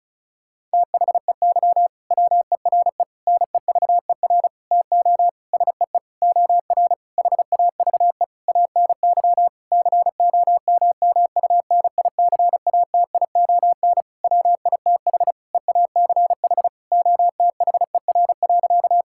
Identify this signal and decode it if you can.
Morse code, heard as a keyed tone, about 36 wpm, 700 Hz. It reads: THEY WERE NEVER TO SEE OR HAVE ANY COMMUNICATION WITH EACH OTHER.
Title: Sentences from Top 1000 Words - Review Part 6 - 35wpm